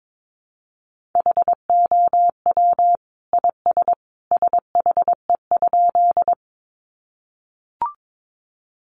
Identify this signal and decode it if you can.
Morse code, heard as a keyed tone, about 22 wpm, 700 Hz. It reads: HOW IS SHE?
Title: Sentences from Top 100 Words 22wpm